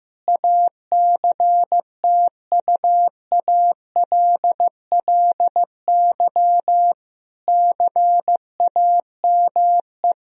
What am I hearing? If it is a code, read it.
Morse code, heard as a keyed tone, about 15 wpm, 700 Hz. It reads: ACTUALLY CAME
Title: Sentences from Top 1000 Words - Review Part 4 - 15wpm